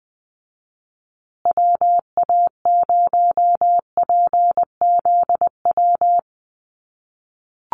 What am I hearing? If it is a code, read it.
Morse code, heard as a keyed tone, about 20 wpm, 700 Hz. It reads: WA0PZW
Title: USA Call Signs Repeated 3x 20wpm